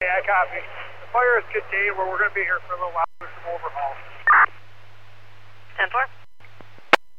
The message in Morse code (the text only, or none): none